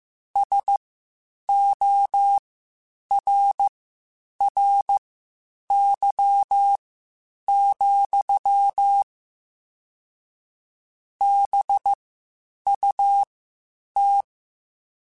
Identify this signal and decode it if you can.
Morse code, heard as a keyed tone, about 15 wpm, spaced as for 8 wpm, 800 Hz. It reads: SORRY, BUT